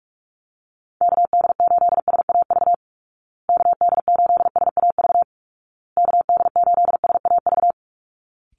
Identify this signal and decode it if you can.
Morse code, heard as a keyed tone, about 44 wpm, 700 Hz. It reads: KD8SAV KD8SAV KD8SAV
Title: Call Signs Repeated 2x - 45wpm